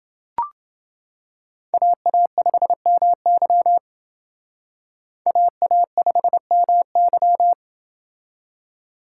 Morse code, heard as a keyed tone, about 29 wpm, 700 Hz. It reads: AA5MY AA5MY